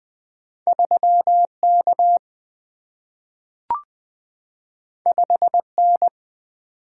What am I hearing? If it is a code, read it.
Morse code, heard as a keyed tone, about 20 wpm, 700 Hz. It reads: 3K 5N
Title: Single-Letter Single-Number 20wpm